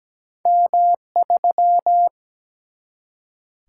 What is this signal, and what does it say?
Morse code, heard as a keyed tone, about 17 wpm, 700 Hz. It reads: M3
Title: Single-Letter Single-Number 17wpm